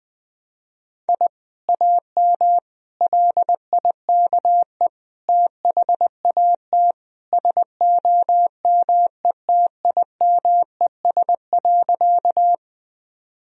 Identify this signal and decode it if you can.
Morse code, heard as a keyed tone, about 20 wpm, 700 Hz. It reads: I AM LIKE THAT SOMETIMES.
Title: Sentences from Top 1000 Words - Review Part 3 - 20wpm